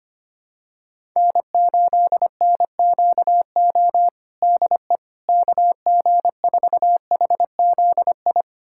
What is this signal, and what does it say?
Morse code, heard as a keyed tone, about 25 wpm, 700 Hz. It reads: N8NQO DE KG4HZI
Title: USA Callsign DE Callsign 25wpm